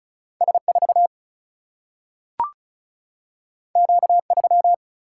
Morse code, heard as a keyed tone, about 36 wpm, 700 Hz. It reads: S4 Q3
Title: Single-Letter Single-Number 35wpm